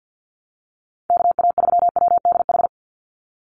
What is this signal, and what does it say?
Morse code, heard as a keyed tone, about 48 wpm, 700 Hz. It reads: KA3WDH
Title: USA Call Signs 50wpm